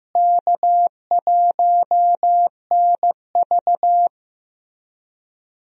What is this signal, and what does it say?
Morse code, heard as a keyed tone, about 15 wpm, 700 Hz. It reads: K1NV
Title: USA Call Signs - 4 Characters - Repeated 3x - 15wpm